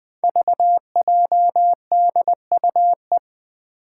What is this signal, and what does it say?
Morse code, heard as a keyed tone, about 20 wpm, 700 Hz. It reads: VJDUE
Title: Five Challenging Letter-Number Combinations 20wpm